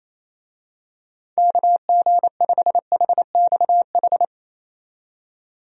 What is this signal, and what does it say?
Morse code, heard as a keyed tone, about 27 wpm, 700 Hz. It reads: KG5HXH
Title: USA Call Signs - 6 Characters - 28wpm